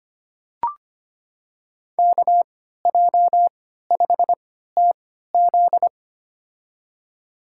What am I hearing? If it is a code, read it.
Morse code, heard as a keyed tone, about 25 wpm, spaced as for 14 wpm, 700 Hz. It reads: KJ5TZ
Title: USA Call Signs - 5 Characters - 3x Character Spacing - 25wpm